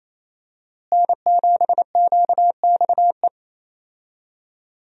Morse code, heard as a keyed tone, about 28 wpm, 700 Hz. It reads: N7QXE
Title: USA Call Signs - 5 Characters - Repeated 2x - 28wpm